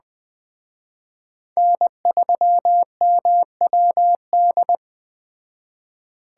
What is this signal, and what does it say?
Morse code, heard as a keyed tone, about 20 wpm, 700 Hz. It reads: N3MWD